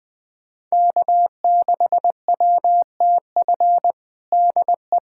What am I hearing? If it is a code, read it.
Morse code, heard as a keyed tone, about 20 wpm, 700 Hz. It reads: K6WTF DE
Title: USA Callsign DE Callsign - 5 Characters - 20wpm